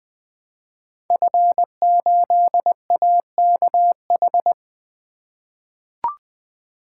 Morse code, heard as a keyed tone, about 20 wpm, 700 Hz. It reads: F8AKH
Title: Code Groups - 5 Letter:Numbers - 20wpm